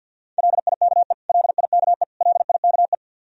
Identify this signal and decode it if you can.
Morse code, heard as a keyed tone, about 53 wpm, 700 Hz. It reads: LIKE LIKE LIKE
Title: Top 100 Words Repeated 50wpm